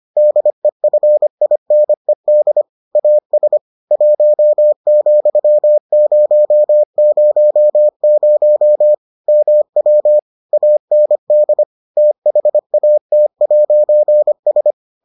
Morse code, heard as a keyed tone, about 25 wpm, 600 Hz. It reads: DEFINED AS 1,000 MW AND THAT'S